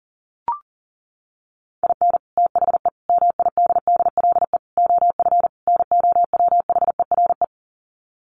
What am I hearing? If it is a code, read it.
Morse code, heard as a keyed tone, about 40 wpm, 700 Hz. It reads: IN THE MIDDLE OF NOWHERE
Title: Sets of 5 Words Encore 40wpm